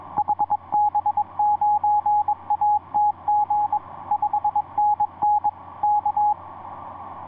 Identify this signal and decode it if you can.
Morse code, heard as a keyed tone, about 22 wpm, 840 Hz, with radio noise. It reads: HB9ATG 5NN K
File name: ce1-k7ca ea8tl etc 17m cw 10jan2010 1620